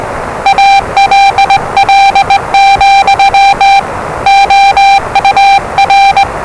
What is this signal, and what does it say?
Morse code, heard as a keyed tone, about 20 wpm, 790 Hz, with noise. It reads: ALL, OUR
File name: WellingtonRadioZlwCloseDownOnHfAndMf / 01-Zlw 4277khz 417.5 qrt until 0925z